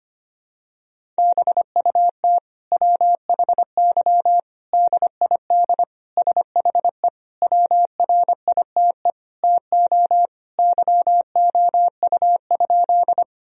Morse code, heard as a keyed tone, about 25 wpm, 700 Hz. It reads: BUT WHY DID SHE WRITE TO YOU?
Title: Sentences from Top 1000 Words - Review Part 1 - 25wpm